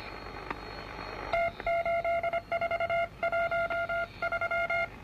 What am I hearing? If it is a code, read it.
Morse code, heard as a keyed tone, about 25 wpm, 680 Hz, with noise. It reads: T8413